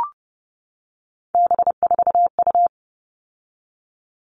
Morse code, heard as a keyed tone, about 30 wpm, 700 Hz. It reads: B4U
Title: All Cut Numbers - 3-Digit 30wpm